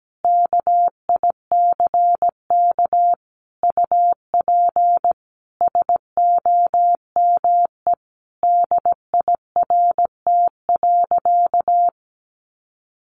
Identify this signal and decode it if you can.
Morse code, heard as a keyed tone, about 17 wpm, 700 Hz. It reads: KICK UP SOME DIRT.
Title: A Sound Of Thunder 17wpm